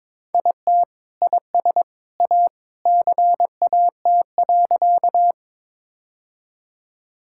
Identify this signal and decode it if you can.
Morse code, heard as a keyed tone, about 22 wpm, 700 Hz. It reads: IT IS A CAT.
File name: practice / 22wpm / Sentences with 3-Word Phrases - 22wpm